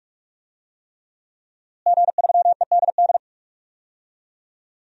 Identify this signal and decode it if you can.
Morse code, heard as a keyed tone, about 42 wpm, 700 Hz. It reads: G3EDD
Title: DXCC-100 Call Signs - 5 Characters - Repeated 3x - 45wpm